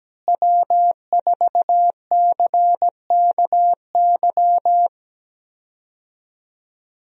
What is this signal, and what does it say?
Morse code, heard as a keyed tone, about 17 wpm, 700 Hz. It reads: W4CKY
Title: USA Call Signs - 5 Characters - Repeated 3x - 17wpm